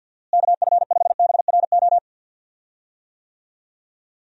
Morse code, heard as a keyed tone, about 49 wpm, 700 Hz. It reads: KF5BRO
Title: USA Call Signs - 6 Characters - Repeated 3x - 50wpm